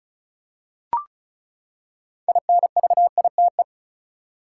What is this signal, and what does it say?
Morse code, heard as a keyed tone, about 36 wpm, 700 Hz. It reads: INVITE
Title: Six Letter Words - 35wpm